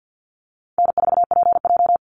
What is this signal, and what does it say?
Morse code, heard as a keyed tone, about 50 wpm, 700 Hz. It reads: N4PJ
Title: Call Signs - 50wpm